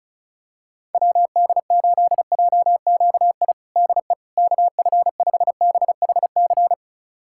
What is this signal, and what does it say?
Morse code, heard as a keyed tone, about 34 wpm, 700 Hz. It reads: WD8JQI DE KF5BHC
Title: USA Callsign DE Callsign 35wpm